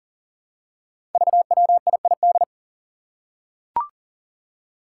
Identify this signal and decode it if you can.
Morse code, heard as a keyed tone, about 40 wpm, 700 Hz. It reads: VWIID
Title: Five Challenging Letter-Number Combinations 40wpm